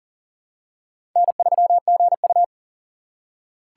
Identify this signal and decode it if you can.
Morse code, heard as a keyed tone, about 40 wpm, 700 Hz. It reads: N3GU
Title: USA Call Signs - 4 Characters - 4x Character Spacing - 40wpm